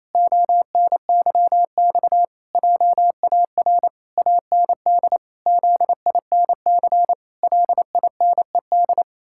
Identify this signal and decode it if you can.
Morse code, heard as a keyed tone, about 28 wpm, 700 Hz. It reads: ONYX JAR AND ZINC LINED
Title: Pangram Challenge v2 - 28wpm